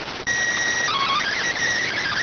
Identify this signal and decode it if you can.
SSTV (slow-scan television), PD 290.